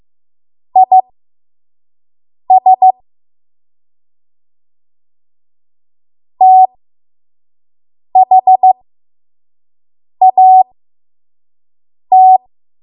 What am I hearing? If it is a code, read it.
Morse code, heard as a keyed tone, about 15 wpm, spaced as for 5 wpm, 750 Hz. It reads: IS THAT
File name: Morse / 05WPM